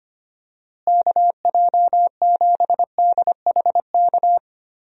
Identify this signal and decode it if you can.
Morse code, heard as a keyed tone, about 25 wpm, 700 Hz. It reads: KJ7DHK
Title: USA Call Signs - 6 Characters - Repeated 2x - 25wpm